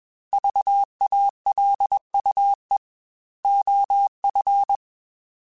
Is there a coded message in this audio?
Yes: Morse code from 0.3 to 4.8 s, heard as a keyed tone, about 21 wpm, 800 Hz. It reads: VALUE OF